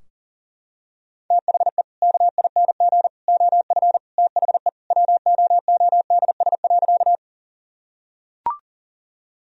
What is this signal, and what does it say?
Morse code, heard as a keyed tone, about 37 wpm, 700 Hz. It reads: THE KING OF THE WOODS.
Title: Sentences from Top 1000 Words - Review Part 2 - 40wpm